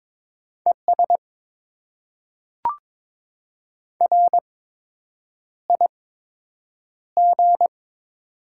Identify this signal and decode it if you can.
Morse code, heard as a keyed tone, about 22 wpm, spaced as for 8 wpm, 700 Hz. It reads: ES RIG
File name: Top 20 QSO Elements - 8x Character Spacing - 22wpm